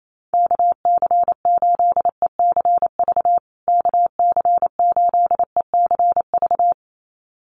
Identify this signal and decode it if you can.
Morse code, heard as a keyed tone, about 28 wpm, 700 Hz. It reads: KC8ECV KC8ECV
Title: USA Call Signs Repeated 2x 28wpm